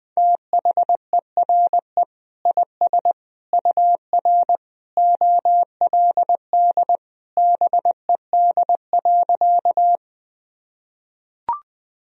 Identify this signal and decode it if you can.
Morse code, heard as a keyed tone, about 20 wpm, 700 Hz. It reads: THERE IS UR OLD BED.
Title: Sentences from Top 500 Words and Abbreviations 20wpm